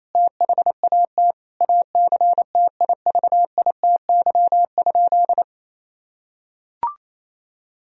Morse code, heard as a keyed tone, about 28 wpm, 700 Hz. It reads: THAT ACTIVITY?
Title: Sentences from Top 1000 Words - Review Part 4 - 28wpm